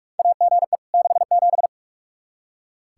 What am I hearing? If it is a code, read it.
Morse code, heard as a keyed tone, about 42 wpm, 700 Hz. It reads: AGE 67
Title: K1USN SST QSO Practice 45wpm